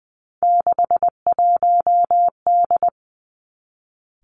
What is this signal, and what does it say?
Morse code, heard as a keyed tone, about 20 wpm, 700 Hz. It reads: 61D
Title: Three Challenging Letter-Number Combinations 20wpm